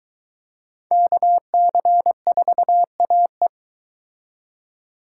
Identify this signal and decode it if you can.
Morse code, heard as a keyed tone, about 22 wpm, 700 Hz. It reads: KC4AE